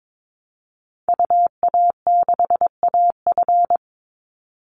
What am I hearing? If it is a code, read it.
Morse code, heard as a keyed tone, about 22 wpm, 700 Hz. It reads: UA6AF
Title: Call Signs Repeated 3x - 22wpm